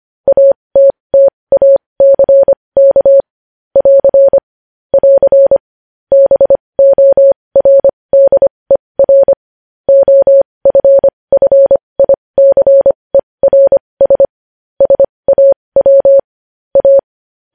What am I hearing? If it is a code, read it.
Morse code, heard as a keyed tone, about 25 wpm, spaced as for 20 wpm, 550 Hz. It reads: ATTACK <AR> <AR> BORDER OFFICERS SAW A
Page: news headlines in morse code at 20 wpm